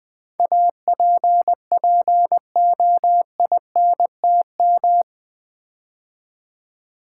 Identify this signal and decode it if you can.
Morse code, heard as a keyed tone, about 20 wpm, 700 Hz. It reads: APPOINTM
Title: Build A Word - 4 Letters At A Time - 11-Letter Words - 20wpm